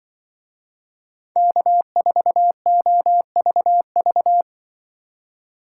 Morse code, heard as a keyed tone, about 24 wpm, 700 Hz. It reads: K4OVV
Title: Speed Racing - USA 1x3 Call Signs 20wpm